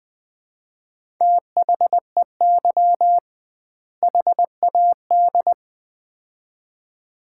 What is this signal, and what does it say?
Morse code, heard as a keyed tone, about 20 wpm, 700 Hz. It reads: THEY HAD
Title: 2x Word-Spacing - Sets of 2 Words - 20wpm